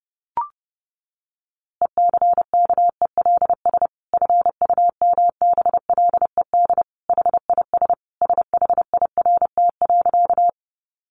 Morse code, heard as a keyed tone, about 30 wpm, 700 Hz. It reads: ECKELS FUMBLED HIS SHIRT.
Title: A Sound Of Thunder 30wpm